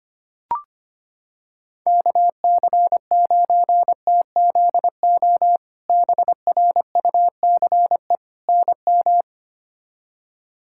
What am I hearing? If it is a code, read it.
Morse code, heard as a keyed tone, about 25 wpm, 700 Hz. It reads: KC9TZO BRUCE NM
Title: K1USN SST QSO Practice - 25wpm